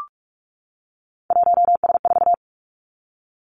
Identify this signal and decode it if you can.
Morse code, heard as a keyed tone, about 46 wpm, 700 Hz. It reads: 1S4
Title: Three Challenging Letter-Number Combinations 45wpm